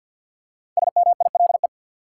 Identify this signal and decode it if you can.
Morse code, heard as a keyed tone, about 49 wpm, 700 Hz. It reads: SMILE
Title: Five Letter Words - 50wpm